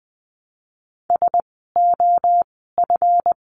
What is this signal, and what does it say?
Morse code, heard as a keyed tone, about 20 wpm, 700 Hz. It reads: S O F
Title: 139 Lesson 23f - Letter f - Words 20wpm